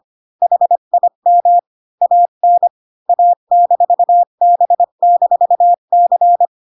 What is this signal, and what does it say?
Morse code, heard as a keyed tone, about 25 wpm, 700 Hz. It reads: HIM AN A-B-C
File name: Pinocchio / Pinocchio ch8 20 25 ik1hge